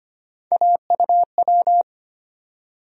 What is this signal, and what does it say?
Morse code, heard as a keyed tone, about 25 wpm, 700 Hz. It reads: AUW
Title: USA Call Sign Postfixes - 3 Letters 25wpm